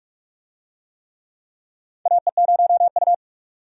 Morse code, heard as a keyed tone, about 46 wpm, 700 Hz. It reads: AE0U
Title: USA Call Signs - 4 Characters - 45wpm